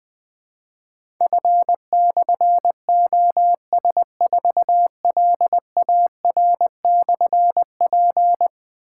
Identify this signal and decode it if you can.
Morse code, heard as a keyed tone, about 20 wpm, 700 Hz. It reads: F/OS4LAR/P
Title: Difficult Call Signs 20wpm